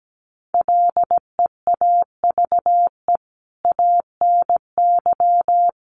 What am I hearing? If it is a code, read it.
Morse code, heard as a keyed tone, about 17 wpm, 700 Hz. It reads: LEAVE ANY
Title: Sentences from Top 400 Words 17wpm